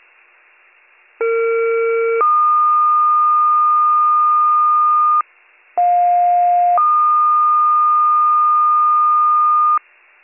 Two-tone page, heard as a one-second tone, then a three-second tone, about 460 then 1220 Hz, about 710 then 1220 Hz. It is heard from 1.2 to 9.8 s.